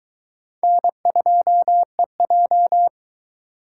Morse code, heard as a keyed tone, about 23 wpm, 700 Hz. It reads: N2EJ